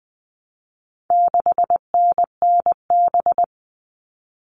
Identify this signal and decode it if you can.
Morse code, heard as a keyed tone, about 20 wpm, 700 Hz. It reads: 6NNB
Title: All Cut Numbers - 4-Digit 20wpm